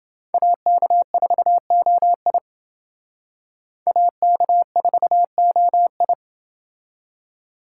Morse code, heard as a keyed tone, about 30 wpm, 700 Hz. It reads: AK4OI AK4OI